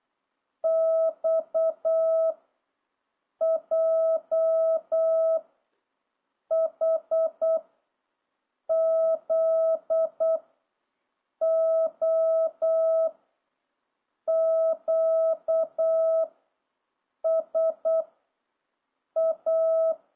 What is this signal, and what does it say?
Morse code, heard as a keyed tone, about 8 wpm, 660 Hz. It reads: X J H Z O Q S A